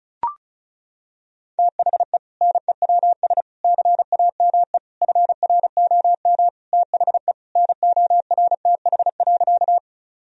Morse code, heard as a keyed tone, about 34 wpm, 700 Hz. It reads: THE NEWS CAME FROM THE NORTH.
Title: Sentences from Top 1000 Words - Review Part 1 - 35wpm